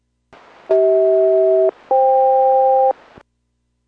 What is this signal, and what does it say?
Aircraft SELCAL, heard as two pairs of tones. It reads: CH-FK